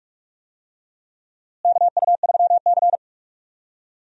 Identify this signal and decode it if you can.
Morse code, heard as a keyed tone, about 46 wpm, 700 Hz. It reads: KU3C
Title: USA Call Signs - 4 Characters - Repeated 3x - 45wpm